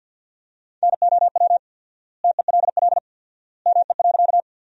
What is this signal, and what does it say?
Morse code, heard as a keyed tone, about 52 wpm, 700 Hz. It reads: NOW TELL ME.
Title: 4x Word-Spacing - Sentences from Top 200 Words - 50wpm